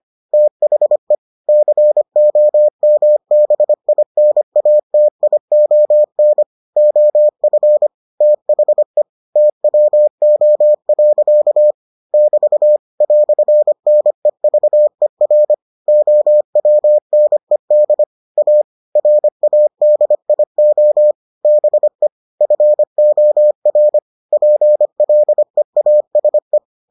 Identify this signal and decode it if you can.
Morse code, heard as a keyed tone, about 25 wpm, 600 Hz. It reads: THE COMBINATION OF THE TWO. <BT> "NEVER OWNED A RADIO BEFOR PLEASE